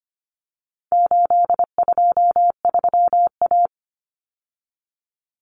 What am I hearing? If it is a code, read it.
Morse code, heard as a keyed tone, about 25 wpm, 700 Hz. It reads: 823A
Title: Common Cut Numbers - 4-Digit 25wpm